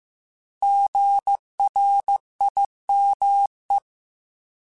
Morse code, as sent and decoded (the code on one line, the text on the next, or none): --. .-. .. -- .
GRIME